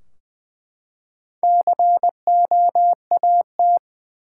Morse code, heard as a keyed tone, about 20 wpm, 700 Hz. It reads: COAT
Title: Four Letter Words - 20wpm